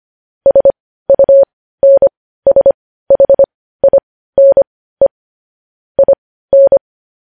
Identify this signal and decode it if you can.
Morse code, heard as a keyed tone, about 25 wpm, spaced as for 15 wpm, 550 Hz. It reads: SUNSHINE IN
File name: podcast / News headlines-1x-25x15